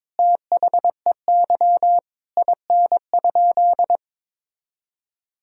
Morse code, heard as a keyed tone, about 22 wpm, 700 Hz. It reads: THEY IN?
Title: Sentences from Top 1000 Words - Review Part 3 - 22wpm